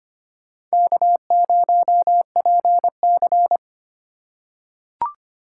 Morse code, heard as a keyed tone, about 25 wpm, 700 Hz. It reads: K0PC
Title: K1USN SST QSO Practice - 25wpm